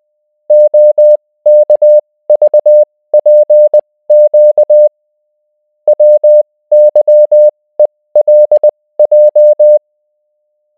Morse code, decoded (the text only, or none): OKVPQ WYELJ